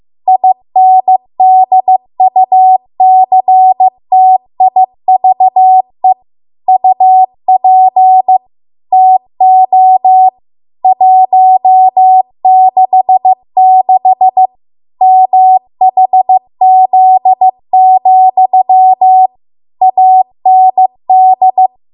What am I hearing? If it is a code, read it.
Morse code, heard as a keyed tone, about 15 wpm, 750 Hz. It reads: INDUCTIVE UP TO 166 MHZ, AND